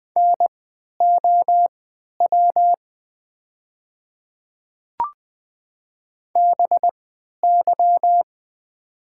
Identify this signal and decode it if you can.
Morse code, heard as a keyed tone, about 20 wpm, spaced as for 11 wpm, 700 Hz. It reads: NOW BY
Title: 3x Character Spacing - Top 100 Words - 20wpm